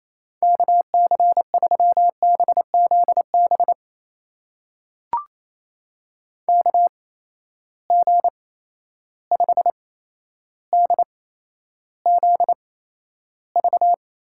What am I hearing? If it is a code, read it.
Morse code, heard as a keyed tone, about 27 wpm, spaced as for 12 wpm, 700 Hz. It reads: KC3BZB KG5DZV
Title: USA Call Signs - 6 Characters - 8x Character Spacing - 28wpm